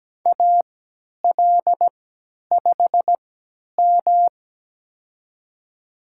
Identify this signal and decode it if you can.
Morse code, heard as a keyed tone, about 17 wpm, spaced as for 10 wpm, 700 Hz. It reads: AL5M
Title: USA Call Signs - 4 Characters - 3x Character Spacing - 17wpm